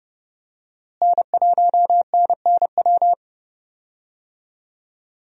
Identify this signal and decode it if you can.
Morse code, heard as a keyed tone, about 30 wpm, 700 Hz. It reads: N1NNW
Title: USA Call Signs - 5 Characters - Repeated 3x - 30wpm